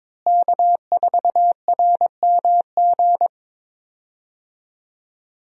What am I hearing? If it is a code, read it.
Morse code, heard as a keyed tone, about 22 wpm, 700 Hz. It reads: K4RMG
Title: USA Call Signs - 5 Characters - Repeated 3x - 22wpm